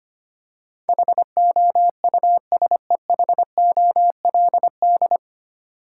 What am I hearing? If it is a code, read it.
Morse code, heard as a keyed tone, about 25 wpm, 690 Hz. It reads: HOUSEHOLD
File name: practice / 25wpm / 9 Letter Words - First 3 Letters Given - 25wpm